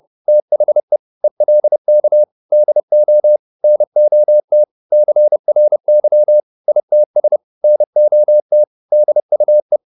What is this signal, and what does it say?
Morse code, heard as a keyed tone, about 30 wpm, 600 Hz. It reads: THE ELK DO NOT CRY ITS NOT DUE